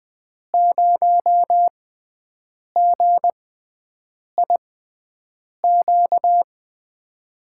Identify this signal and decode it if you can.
Morse code, heard as a keyed tone, about 20 wpm, spaced as for 7 wpm, 700 Hz. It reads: 0GIQ
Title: USA Call Signs - 6 Characters - 6x Character Spacing - 20wpm